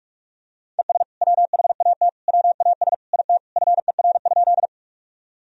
Morse code, heard as a keyed tone, about 46 wpm, 700 Hz. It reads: ES WHAT WAS IT FER?
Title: Sentences from Top 100 Words and Abbreviations 45wpm